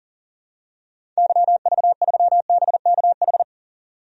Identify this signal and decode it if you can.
Morse code, heard as a keyed tone, about 40 wpm, 700 Hz. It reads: YV3BKH
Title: DXCC-100 Call Signs - 6 Characters - 40wpm